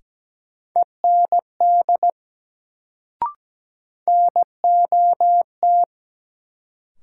Morse code, heard as a keyed tone, about 17 wpm, 700 Hz. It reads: END NOT
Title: Sets of 2 Words - New Word Drill - 17wpm